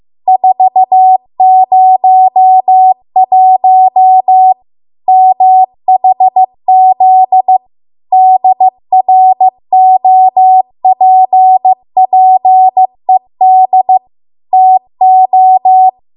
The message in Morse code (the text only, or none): 401 MHZ DROPPED TO